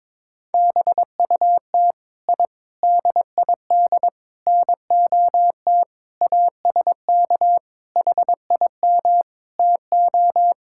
Morse code, heard as a keyed tone, about 22 wpm, 700 Hz. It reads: BUT I DID NOT ASK HIM TO
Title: Sentences from Top 1000 Words - Review Part 5 - 22wpm